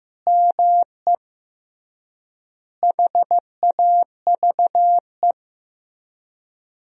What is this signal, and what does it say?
Morse code, heard as a keyed tone, about 15 wpm, 700 Hz. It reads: ME HAVE
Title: 3x Word-Spacing - Sentences from Top 300 Words - 15wpm